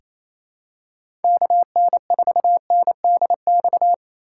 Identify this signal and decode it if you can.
Morse code, heard as a keyed tone, about 28 wpm, 700 Hz. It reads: KN4NDX
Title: USA Call Signs - 8x Character Spacing - 28wpm